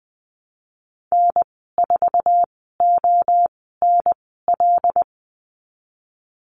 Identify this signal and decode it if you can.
Morse code, heard as a keyed tone, about 20 wpm, spaced as for 15 wpm, 700 Hz. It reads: N4ONL
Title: 128 Lesson 21g - Number 6 - Call Signs 20wpm